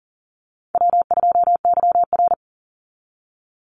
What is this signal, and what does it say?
Morse code, heard as a keyed tone, about 40 wpm, 700 Hz. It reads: W2YR
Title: Call Signs Repeated 2x - 40wpm